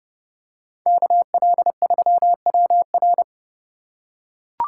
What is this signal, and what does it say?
Morse code, heard as a keyed tone, about 30 wpm, 700 Hz. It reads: KL3WR
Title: USA Call Signs - 5 Characters - 2x Character Spacing - 30wpm